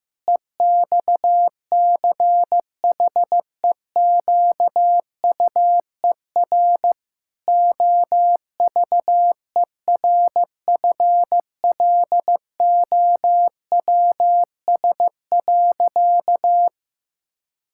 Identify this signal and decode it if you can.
Morse code, heard as a keyed tone, about 15 wpm, 700 Hz. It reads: EXCHEQUER OVERFLOWS.
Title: Pangram Challenge v2 - 15wpm